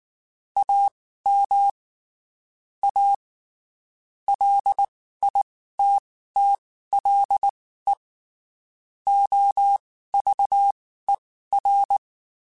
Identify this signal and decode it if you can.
Morse code, heard as a keyed tone, about 19 wpm, spaced as for 14 wpm, 800 Hz. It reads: AM A LITTLE OVER
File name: (spaced) / QSOs Spaced-mp3s / WA7QOX 2 @18 sp